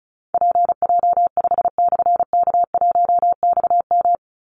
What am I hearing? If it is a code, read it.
Morse code, heard as a keyed tone, about 34 wpm, 700 Hz. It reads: PJ5/K1XM
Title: Call Signs Repeated 3x - 35wpm